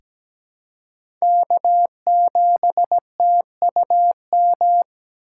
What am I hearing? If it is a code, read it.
Morse code, heard as a keyed tone, about 17 wpm, 700 Hz. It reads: K7TUM
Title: USA Call Signs - 5 Characters - 3x Character Spacing - 17wpm